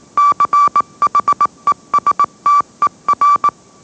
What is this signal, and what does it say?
Morse code, heard as a keyed tone, about 18 wpm, 1160 Hz. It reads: CHESTER